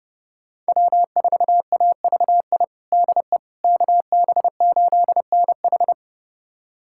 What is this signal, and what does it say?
Morse code, heard as a keyed tone, about 30 wpm, 700 Hz. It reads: W4AVI DE KB8NH
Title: USA Callsign DE Callsign - 5 Characters - 30wpm